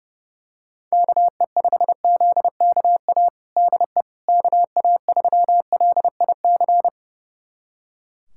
Morse code, heard as a keyed tone, about 30 wpm, 700 Hz. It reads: KE5ZKA DE KA3LIC
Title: USA Callsign DE Callsign 30wpm